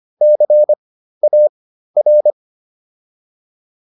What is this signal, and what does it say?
Morse code, heard as a keyed tone, about 25 wpm, spaced as for 13 wpm, 600 Hz. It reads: CAR